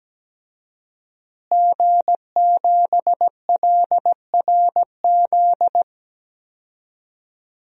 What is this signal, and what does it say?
Morse code, heard as a keyed tone, about 17 wpm, 700 Hz. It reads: G7LRZ
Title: DXCC-100 Call Signs - 5 Characters - Repeated 2x - 17wpm